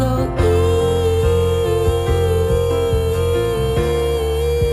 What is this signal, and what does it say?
Busy tone (China, 450 Hz), with noise.